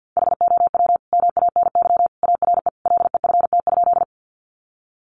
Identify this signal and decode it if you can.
Morse code, heard as a keyed tone, about 50 wpm, 700 Hz. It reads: HOW MANY ARE LEFT?
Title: Sentences from Top 300 Words 50wpm